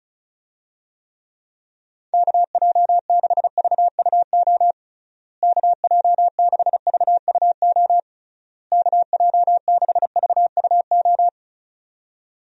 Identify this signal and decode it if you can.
Morse code, heard as a keyed tone, about 36 wpm, 700 Hz. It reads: KJ6VUO KJ6VUO KJ6VUO
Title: USA Call Signs - 6 Characters - Repeated 3x - 35wpm